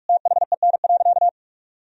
Morse code, heard as a keyed tone, about 46 wpm, 700 Hz. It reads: THEN.
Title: Sentences from Top 1000 Words - Review Part 3 - 45wpm